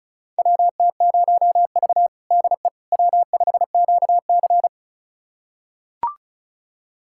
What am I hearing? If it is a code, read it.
Morse code, heard as a keyed tone, about 34 wpm, 700 Hz. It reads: WT0V DE W5QC
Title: USA Callsign DE Callsign - 4 Characters - 35wpm